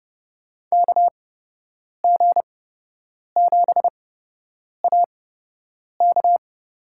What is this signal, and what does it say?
Morse code, heard as a keyed tone, about 30 wpm, spaced as for 8 wpm, 700 Hz. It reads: KG7AK
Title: USA Call Signs - 6 Characters - 8x Character Spacing - 30wpm